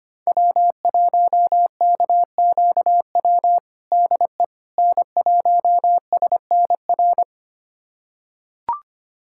Morse code, heard as a keyed tone, about 25 wpm, 700 Hz. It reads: W1KQW DE N1SNR
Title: USA Callsign DE Callsign - 5 Characters - 25wpm